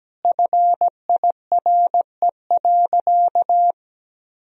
Morse code, heard as a keyed tone, about 17 wpm, 700 Hz. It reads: FIRE.
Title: Sentences from Top 1000 Words - Review Part 1 - 17wpm